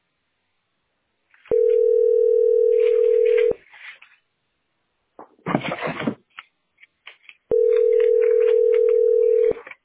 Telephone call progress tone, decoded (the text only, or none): ringback tone